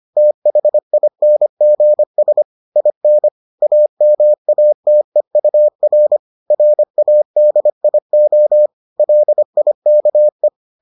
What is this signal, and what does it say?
Morse code, heard as a keyed tone, about 25 wpm, 600 Hz. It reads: THINGS IN AMATEUR RADIO LIKE